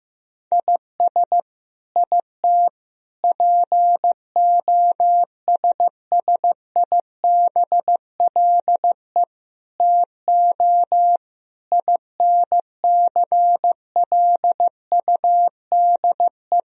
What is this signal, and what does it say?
Morse code, heard as a keyed tone, about 15 wpm, 700 Hz. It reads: IS IT POSSIBLE TO INCLUDE
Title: Sentences from Top 1000 Words - Review Part 3 - 15wpm